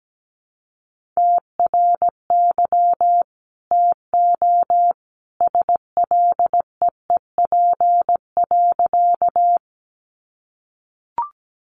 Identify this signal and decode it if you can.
Morse code, heard as a keyed tone, about 17 wpm, 700 Hz. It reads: TRY TO SLEEP.
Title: Sentences from Top 500 Words 17wpm